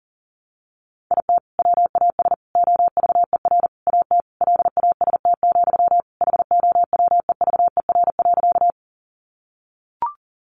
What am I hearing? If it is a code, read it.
Morse code, heard as a keyed tone, about 40 wpm, 700 Hz. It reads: IT WAS OVER AT LAST, HOWEVER.
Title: Sentences from Top 600 Words 40wpm